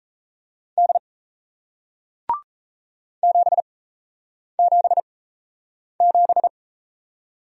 Morse code, heard as a keyed tone, about 38 wpm, 700 Hz. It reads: D 7 7 7